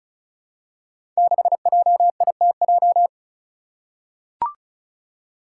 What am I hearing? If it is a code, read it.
Morse code, heard as a keyed tone, about 36 wpm, 700 Hz. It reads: 6JITJ
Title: Five Challenging Letter-Number Combinations 35wpm